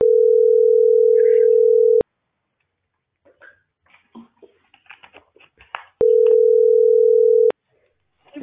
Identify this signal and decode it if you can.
Ringback tone (North America), 440 + 480 Hz.